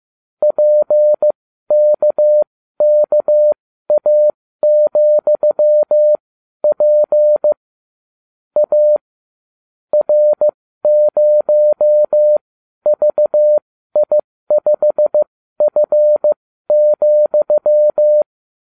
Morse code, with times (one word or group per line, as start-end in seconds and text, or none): 0.4-7.5 PKKA,P
8.6-9.0 A
9.9-18.2 R0VI5F,